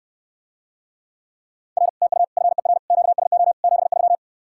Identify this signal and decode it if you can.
Morse code, heard as a keyed tone, about 69 wpm, 700 Hz. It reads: R TU FER QSO 73